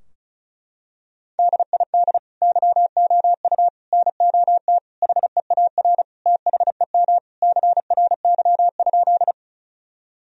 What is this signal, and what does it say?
Morse code, heard as a keyed tone, about 36 wpm, 700 Hz. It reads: DID YOU NOT HEAR THEM CRY?
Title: Sentences from Top 1000 Words - Review Part 2 - 35wpm